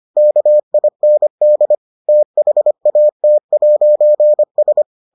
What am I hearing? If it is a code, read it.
Morse code, heard as a keyed tone, about 25 wpm, 600 Hz. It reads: KIND THAT'S